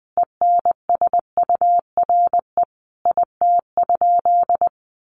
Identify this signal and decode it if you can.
Morse code, heard as a keyed tone, about 20 wpm, 700 Hz. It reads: ENSURE IT?
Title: Sentences from Top 1100 Words 20wpm